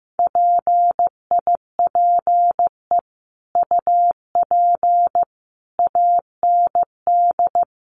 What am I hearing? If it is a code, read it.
Morse code, heard as a keyed tone, about 14 wpm, 700 Hz. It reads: PIPE UP AND
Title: Sentences from Top 1500 Words 15wpm